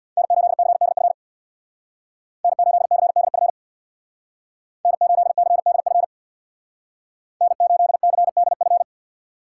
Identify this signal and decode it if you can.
Morse code, heard as a keyed tone, about 59 wpm, 700 Hz. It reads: N8CDF N8CDF N8CDF N8CDF
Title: Speed Racing - USA 1x3 Call Signs 50wpm